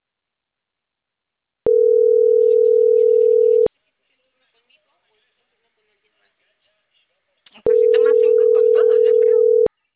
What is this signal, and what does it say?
Ringback tone (North America), 440 + 480 Hz.